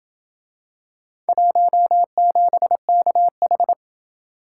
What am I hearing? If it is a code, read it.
Morse code, heard as a keyed tone, about 27 wpm, 700 Hz. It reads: 17KH